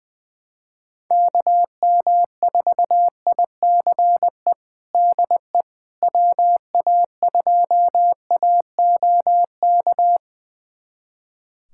Morse code, heard as a keyed tone, about 20 wpm, 700 Hz. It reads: KM4ICE DE WA2AOK
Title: USA Callsign DE Callsign - 6 Characters - 20wpm